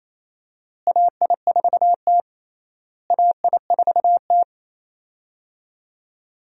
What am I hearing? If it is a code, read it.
Morse code, heard as a keyed tone, about 27 wpm, 700 Hz. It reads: AI4T AI4T
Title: USA Call Signs - 4 Characters - Repeated 3x - 28wpm